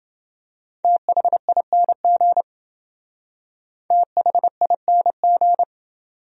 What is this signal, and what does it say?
Morse code, heard as a keyed tone, about 28 wpm, 700 Hz. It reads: THING THING